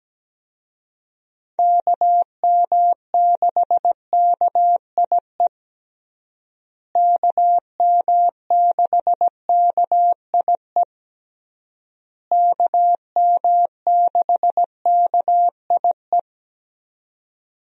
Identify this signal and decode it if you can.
Morse code, heard as a keyed tone, about 17 wpm, 700 Hz. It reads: KM6KIE KM6KIE KM6KIE
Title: USA Call Signs - 6 Characters - Repeated 3x - 17wpm